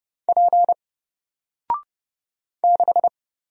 Morse code, heard as a keyed tone, about 30 wpm, 700 Hz. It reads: P 6
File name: Single-Letter-Number-30wpm